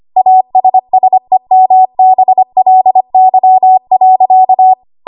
Morse code, heard as a keyed tone, about 25 wpm, 750 Hz. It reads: ASSEMBLY.
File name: Morse / 25WPM